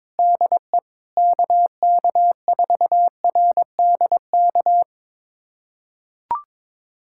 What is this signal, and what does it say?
Morse code, heard as a keyed tone, about 22 wpm, 700 Hz. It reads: DE KK4RDK
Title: USA Callsign DE Callsign 22wpm